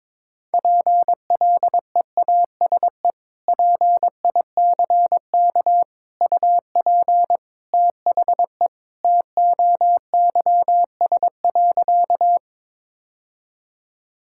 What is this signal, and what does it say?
Morse code, heard as a keyed tone, about 22 wpm, 700 Hz. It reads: PLEASE PICK UP THE TOYS.
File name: practice / 22wpm / Sentences with 2-Word Phrases - 22wpm